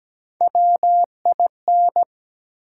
Morse code, heard as a keyed tone, about 17 wpm, 700 Hz. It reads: WIN
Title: Sentences from Top 1500 Words - New Word Drill - 17wpm